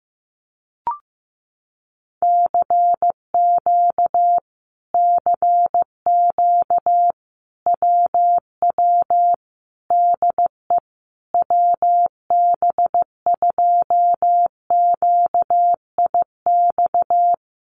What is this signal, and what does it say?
Morse code, heard as a keyed tone, about 15 wpm, 700 Hz. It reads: CQ CQ WW DE WB2QIX
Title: Common QSO Phrases Encore 15wpm